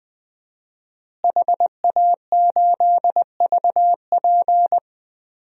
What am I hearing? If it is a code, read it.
Morse code, heard as a keyed tone, about 20 wpm, 700 Hz. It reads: HA8VP
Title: DXCC-100 Call Signs - 5 Characters - Repeated 3x - 20wpm